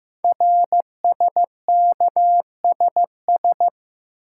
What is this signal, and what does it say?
Morse code, heard as a keyed tone, about 15 wpm, 700 Hz. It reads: RSKSS
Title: Code Groups - 5 Letters - 15wpm